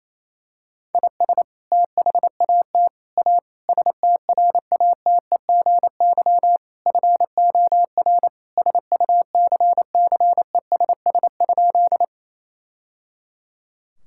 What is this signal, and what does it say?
Morse code, heard as a keyed tone, about 28 wpm, 700 Hz. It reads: IS THAT A STRATEGY FOR SUCCESS?
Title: Sentences from Top 1000 Words - Review Part 5 - 28wpm